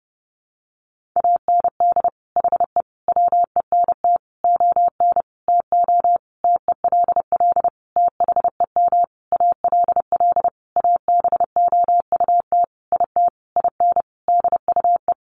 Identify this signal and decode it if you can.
Morse code, heard as a keyed tone, about 30 wpm, 700 Hz. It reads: AND HE WENT ON TO TELL THEM ALL ABOUT IT IN DUE
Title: Sentences from Top 900 Words 30wpm